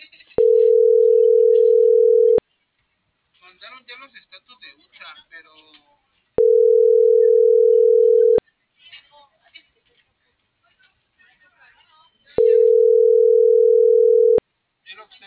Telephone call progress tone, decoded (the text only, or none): ringback tone